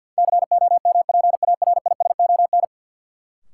Morse code, heard as a keyed tone, about 53 wpm, 700 Hz. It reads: COMPARISON
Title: Sentences from Top 1600 Words - New Word Drill - 50wpm